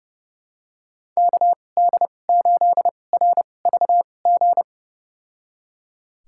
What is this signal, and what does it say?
Morse code, heard as a keyed tone, about 31 wpm, spaced as for 22 wpm, 700 Hz. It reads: KD8RVG
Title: USA Call Signs - 6 Characters - 2x Character Spacing - 30wpm